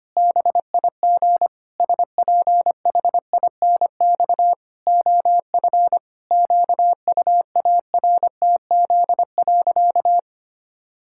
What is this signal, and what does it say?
Morse code, heard as a keyed tone, about 25 wpm, 700 Hz. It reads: BIG SPHINX OF QUARTZ.
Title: Pangram Challenge v2 - 25wpm